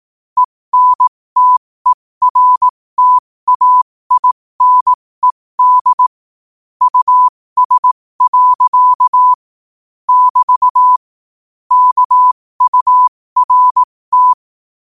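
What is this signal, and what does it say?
Morse code, heard as a keyed tone, about 18 wpm, spaced as for 15 wpm, 1000 Hz. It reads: ENTERTAINED US. <BT> KURT
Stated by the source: Morse quote-of-the-day practice at 15 WPM from Morse Resource.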